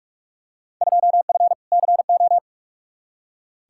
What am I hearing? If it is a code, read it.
Morse code, heard as a keyed tone, about 41 wpm, 700 Hz. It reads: 2F CO
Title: Field Day Exchanges - 45wpm